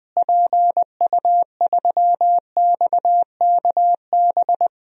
Morse code, heard as a keyed tone, about 20 wpm, 700 Hz. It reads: PU3XKB
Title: DXCC-100 Call Signs - 6 Characters - Repeated 2x - 20wpm